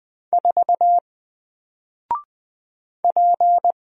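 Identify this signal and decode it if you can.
Morse code, heard as a keyed tone, about 20 wpm, 700 Hz. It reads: 4 P
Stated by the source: Single letter and number Morse drill at 20 wpm, from N4LCD.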